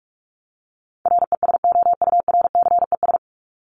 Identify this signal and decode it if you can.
Morse code, heard as a keyed tone, about 44 wpm, 700 Hz. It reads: RESOURCES
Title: ES:S Suffix - Top 100 Words with Base Words 45wpm